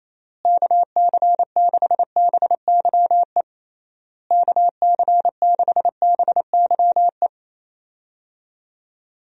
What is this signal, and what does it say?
Morse code, heard as a keyed tone, about 28 wpm, 700 Hz. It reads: KC6BYE KC6BYE
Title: USA Call Signs - 6 Characters - Repeated 3x - 28wpm